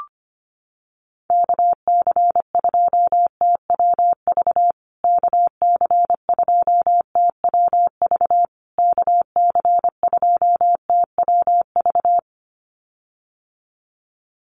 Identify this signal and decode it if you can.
Morse code, heard as a keyed tone, about 25 wpm, 700 Hz. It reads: KC2TWV KC2TWV KC2TWV
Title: USA Call Signs Repeated 3x 25wpm